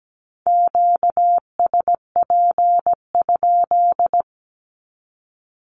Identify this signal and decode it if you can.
Morse code, heard as a keyed tone, about 17 wpm, 700 Hz. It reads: QSP?
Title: Q-Codes with Spoken Meaning - 17wpm